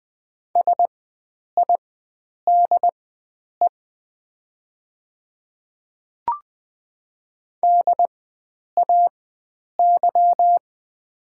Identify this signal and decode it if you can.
Morse code, heard as a keyed tone, about 20 wpm, spaced as for 9 wpm, 700 Hz. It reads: SIDE DAY
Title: 4x Character Spacing - Top 300 Words - 20wpm